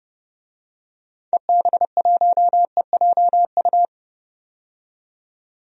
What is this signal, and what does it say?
Morse code, heard as a keyed tone, about 30 wpm, 700 Hz. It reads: EB1EJU
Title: DXCC-100 Call Signs - 6 Characters - 30wpm